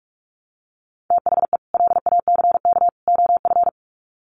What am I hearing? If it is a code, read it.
Morse code, heard as a keyed tone, about 44 wpm, 700 Hz. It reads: THE LACK OF
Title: Sets of 3 Words 45wpm